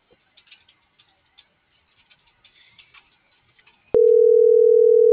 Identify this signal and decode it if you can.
Ringback tone (North America), 440 + 480 Hz.